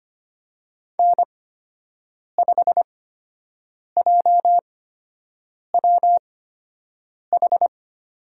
Morse code, heard as a keyed tone, about 25 wpm, spaced as for 7 wpm, 700 Hz. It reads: N5JWH
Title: USA Call Signs - 5 Characters - 8x Character Spacing - 25wpm